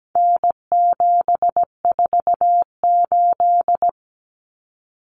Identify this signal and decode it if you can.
Morse code, heard as a keyed tone, about 17 wpm, 700 Hz. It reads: N748
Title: Common Cut Numbers - 4-Digit 17wpm